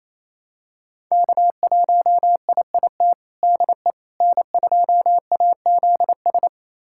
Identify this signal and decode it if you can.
Morse code, heard as a keyed tone, about 28 wpm, 700 Hz. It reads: K1IIT DE N2AZS
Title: USA Callsign DE Callsign - 5 Characters - 28wpm